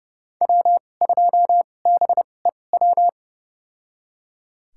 Morse code, heard as a keyed tone, about 30 wpm, spaced as for 22 wpm, 700 Hz. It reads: W2BEW
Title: USA Call Signs - 5 Characters - 2x Character Spacing - 30wpm